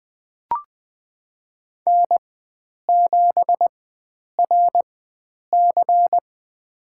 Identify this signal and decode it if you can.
Morse code, heard as a keyed tone, about 20 wpm, spaced as for 9 wpm, 700 Hz. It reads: N7RC
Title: USA Call Signs - 4 Characters - 4x Character Spacing - 20wpm